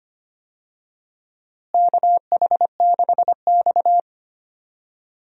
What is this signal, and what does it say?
Morse code, heard as a keyed tone, about 25 wpm, 700 Hz. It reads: KH6X
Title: DXCC-100 Call Signs - 4 Characters - Repeated 2x - 25wpm